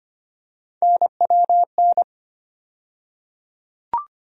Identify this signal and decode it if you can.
Morse code, heard as a keyed tone, about 25 wpm, 700 Hz. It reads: NWN
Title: USA Call Sign Postfixes - 3 Letters 25wpm